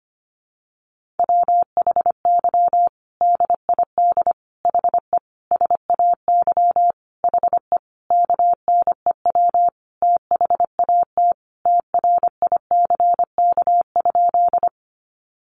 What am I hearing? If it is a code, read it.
Morse code, heard as a keyed tone, about 25 wpm, 700 Hz. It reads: WHY DID HE SAY HE KNEW THAT TRICK?
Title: Sentences from Top 1500 Words 25wpm